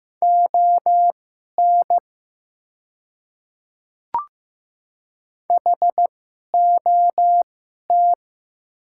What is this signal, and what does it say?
Morse code, heard as a keyed tone, about 15 wpm, spaced as for 11 wpm, 700 Hz. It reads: ON HOT
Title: 2x Character Spacing - Top 500 Words - 15wpm